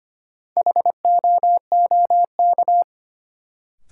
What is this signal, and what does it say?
Morse code, heard as a keyed tone, about 25 wpm, 700 Hz. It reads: HOOK
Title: Sentences from Top 1600 Words - New Word Drill Alt - 25wpm